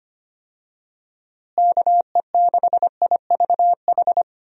Morse code, heard as a keyed tone, about 25 wpm, 700 Hz. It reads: KE6IVH
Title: USA Call Signs - 6 Characters - Repeated 2x - 25wpm